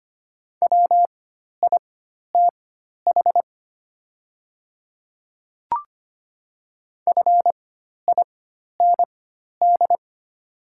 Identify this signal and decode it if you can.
Morse code, heard as a keyed tone, about 25 wpm, spaced as for 12 wpm, 700 Hz. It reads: WITH FIND
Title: 4x Character Spacing - Top 100 Words - 25wpm